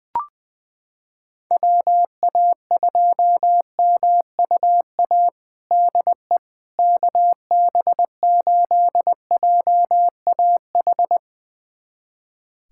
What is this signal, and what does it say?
Morse code, heard as a keyed tone, about 20 wpm, 700 Hz. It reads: WA2MUA DE KB8JAH
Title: USA Callsign DE Callsign 20wpm